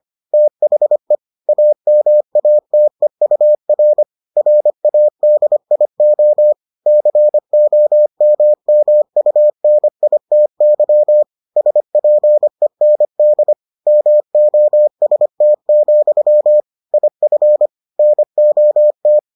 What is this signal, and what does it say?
Morse code, heard as a keyed tone, about 25 wpm, 600 Hz. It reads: THE AMATEUR RADIO COMMUNITY SPEND MOST, IF NOT